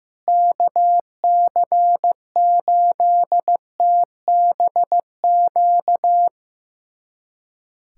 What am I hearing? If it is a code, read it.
Morse code, heard as a keyed tone, about 15 wpm, 700 Hz. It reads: KC8TBQ
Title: USA Callsign DE Callsign 15wpm